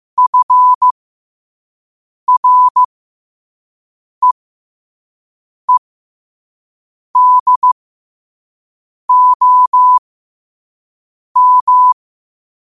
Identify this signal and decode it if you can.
Morse code, heard as a keyed tone, about 15 wpm, spaced as for 5 wpm, 1000 Hz. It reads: FREEDOM
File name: qotdfiles / 5wpm